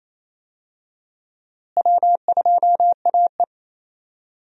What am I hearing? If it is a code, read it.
Morse code, heard as a keyed tone, about 28 wpm, 700 Hz. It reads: W2AE
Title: USA Call Signs - 4 Characters - Repeated 2x - 28wpm